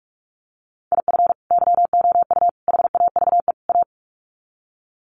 Morse code, heard as a keyed tone, about 44 wpm, 700 Hz. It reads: IF YOU HAVE A
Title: Sets of 4 Words 45wpm